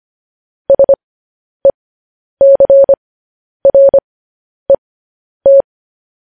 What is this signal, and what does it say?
Morse code, heard as a keyed tone, about 25 wpm, spaced as for 10 wpm, 550 Hz. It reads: SECRET
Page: news headlines in morse code at 10 wpm